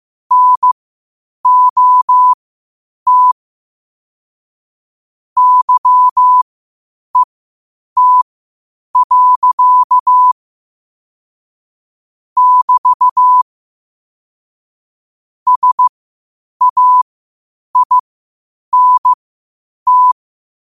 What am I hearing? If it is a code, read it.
Morse code, heard as a keyed tone, about 15 wpm, spaced as for 8 wpm, 1000 Hz. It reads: NOT YET. <BT> SAINT
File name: qotdfiles / 8wpm